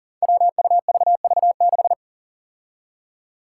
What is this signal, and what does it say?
Morse code, heard as a keyed tone, about 40 wpm, 700 Hz. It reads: WUVV6
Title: Five Challenging Letter-Number Combinations 40wpm